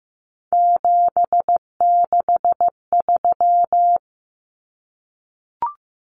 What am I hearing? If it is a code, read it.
Morse code, heard as a keyed tone, about 15 wpm, 700 Hz. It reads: 763
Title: Numbers - 3-Digit 15wpm